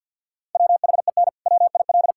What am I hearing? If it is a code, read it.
Morse code, heard as a keyed tone, about 52 wpm, 700 Hz. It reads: WHEN WIL
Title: Sentences from Top 1000 Words - Review Part 5 - 50wpm